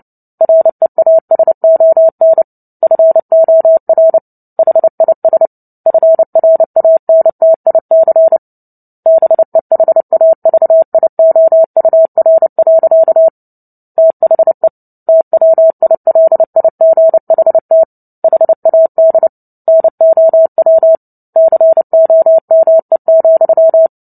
Morse code, heard as a keyed tone, about 30 wpm, 650 Hz. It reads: REASON FOR HIS FRANTIC BEHAVIOUR. THE TWILIGHT HAD NOW COME,
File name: Book Two / 03-Chapter 3 - 29 WPM